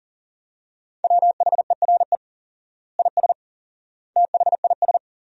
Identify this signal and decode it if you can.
Morse code, heard as a keyed tone, about 40 wpm, 700 Hz. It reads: WHERE IS THIS
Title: 4x Word-Spacing - Sentences from Top 300 Words - 40wpm